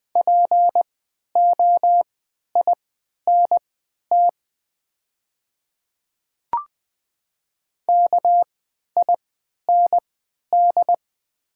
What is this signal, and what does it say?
Morse code, heard as a keyed tone, about 20 wpm, spaced as for 11 wpm, 690 Hz. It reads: POINT KIND
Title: 3x Character Spacing - Top 300 Words - 20wpm